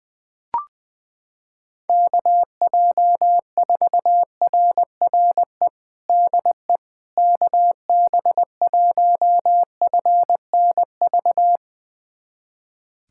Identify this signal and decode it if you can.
Morse code, heard as a keyed tone, about 20 wpm, 700 Hz. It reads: KJ4RRE DE KB1FNV
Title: USA Callsign DE Callsign - 6 Characters - 20wpm